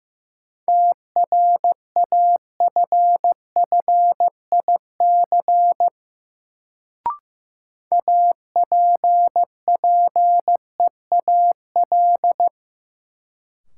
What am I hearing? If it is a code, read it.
Morse code, heard as a keyed tone, about 15 wpm, 700 Hz. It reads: TRAFFIC APPEAL
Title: Sentences from Top 1300 Words - New Word Drill Alt - 15wpm